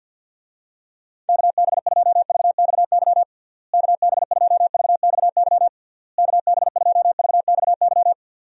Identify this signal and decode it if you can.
Morse code, heard as a keyed tone, about 50 wpm, 700 Hz. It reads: KB2VXY KB2VXY KB2VXY
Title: USA Call Signs - 6 Characters - Repeated 3x - 50wpm